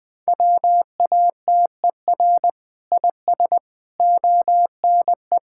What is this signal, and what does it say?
Morse code, heard as a keyed tone, about 21 wpm, 700 Hz. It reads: WATER IS ONE
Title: Sentences with 5-Word Phrases - Spoken Beginning - 20wpm